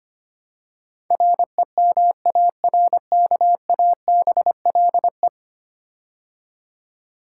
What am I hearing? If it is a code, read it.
Morse code, heard as a keyed tone, about 25 wpm, 700 Hz. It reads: REMARKABLE
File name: 10 Letter Words - First 3 Letters Given - 25wpm